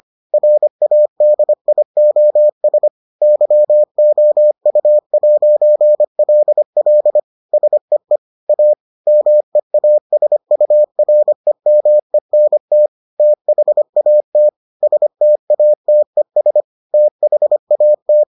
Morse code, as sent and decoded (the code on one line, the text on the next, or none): .-. .- -.. .. --- ... / -.-- --- ..- .----. .-.. .-.. / ... . / .- / -- . .- ... ..- .-. -- . -. - / - .... .- - / ... - .- - . ... / - .... .- -
RADIOS YOU'LL SEE A MEASUREMENT THAT STATES THAT